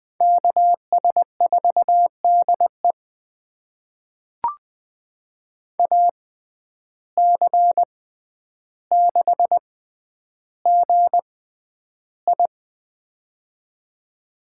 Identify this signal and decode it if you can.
Morse code, heard as a keyed tone, about 20 wpm, spaced as for 10 wpm, 700 Hz. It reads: KS4DE AC6GI
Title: USA Call Signs - 5 Characters - 6x Character Spacing - 20wpm